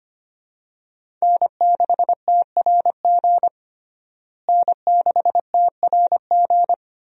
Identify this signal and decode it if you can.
Morse code, heard as a keyed tone, about 25 wpm, 700 Hz. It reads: N6TRG N6TRG
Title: USA Call Signs - 5 Characters - Repeated 3x - 25wpm